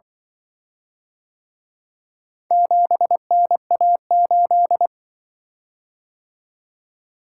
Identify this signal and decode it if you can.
Morse code, heard as a keyed tone, about 23 wpm, 700 Hz. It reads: 7NA8
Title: Speed-Racing - Four Letter-Number Combinations - 20wpm